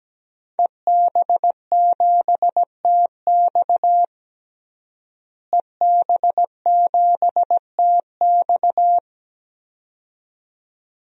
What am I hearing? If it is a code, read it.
Morse code, heard as a keyed tone, about 17 wpm, 700 Hz. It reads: EB7TX EB7TX
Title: DXCC-100 Call Signs - 5 Characters - Repeated 2x - 17wpm